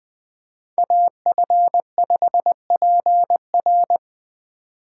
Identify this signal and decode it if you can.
Morse code, heard as a keyed tone, about 20 wpm, 700 Hz. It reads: AF5PR
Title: USA Call Signs - 3x Character Spacing - 20wpm